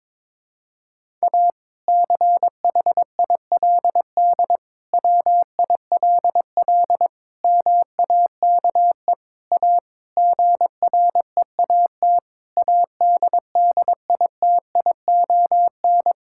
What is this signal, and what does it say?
Morse code, heard as a keyed tone, about 22 wpm, 700 Hz. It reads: A CHILD WILL MAKE A GREAT ADDITION
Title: Sentences from Top 1000 Words - Review Part 6 - 22wpm